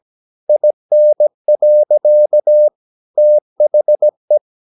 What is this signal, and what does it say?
Morse code, heard as a keyed tone, about 17 wpm, 600 Hz. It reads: IN. THE
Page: Romanen Never af Ken Follet på engelsk.